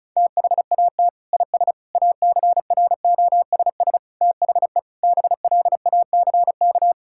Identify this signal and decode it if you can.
Morse code, heard as a keyed tone, about 36 wpm, 690 Hz. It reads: THAT IS ACROSS THE BLACK
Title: Sentences from Top 1000 Words - Review Part 5 - 35wpm